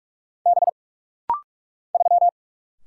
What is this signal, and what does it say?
Morse code, heard as a keyed tone, about 45 wpm, 700 Hz. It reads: B 3
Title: Single Letter-Number - Rapid-Fire 45wpm